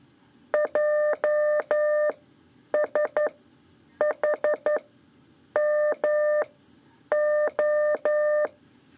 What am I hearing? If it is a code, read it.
Morse code, heard as a keyed tone, about 11 wpm, spaced as for 8 wpm, 600 Hz. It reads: JSHMO
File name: 3 lezione CW 2 AIR